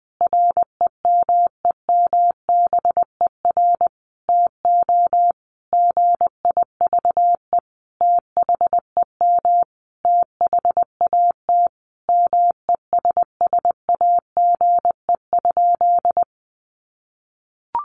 Morse code, heard as a keyed tone, about 20 wpm, 700 Hz. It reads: REMEMBER TO GIVE THEM THAT MESSAGE?
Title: Sentences from Top 1100 Words 20wpm